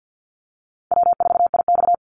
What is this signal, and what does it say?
Morse code, heard as a keyed tone, about 50 wpm, 700 Hz. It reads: W4IX
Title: Easy Call Signs - 50wpm